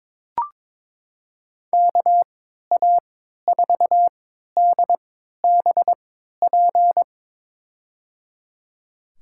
Morse code, heard as a keyed tone, about 22 wpm, spaced as for 13 wpm, 700 Hz. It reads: KA4DBP
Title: USA Call Signs - 6 Characters - 3x Character Spacing - 22wpm